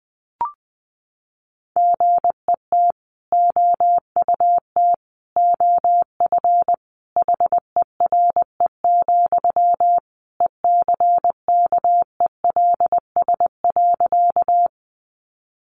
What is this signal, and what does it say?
Morse code, heard as a keyed tone, about 20 wpm, 700 Hz. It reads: GET OUT OF HERE, ECKELS.
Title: A Sound Of Thunder 20wpm